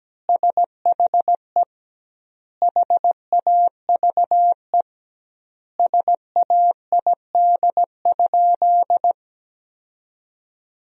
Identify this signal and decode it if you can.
Morse code, heard as a keyed tone, about 17 wpm, 700 Hz. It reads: SHE HAVE SAID?
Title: 2x Word-Spacing - Sentences from Top 100 Words - 17wpm